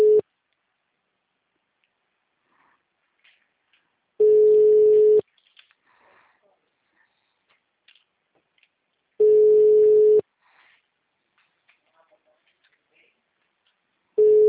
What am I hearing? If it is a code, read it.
Ringback tone (Europe, 425 Hz), with speech.
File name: out- -2004- - - .300